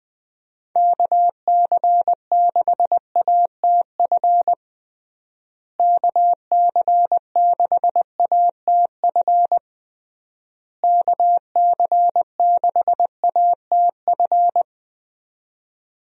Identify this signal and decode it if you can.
Morse code, heard as a keyed tone, about 20 wpm, 700 Hz. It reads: KC6ATF KC6ATF KC6ATF
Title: USA Call Signs - 6 Characters - Repeated 3x - 20wpm